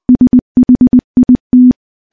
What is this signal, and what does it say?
Morse code, heard as a keyed tone, about 20 wpm, 250 Hz. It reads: SHIT
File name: morse-training-20WPM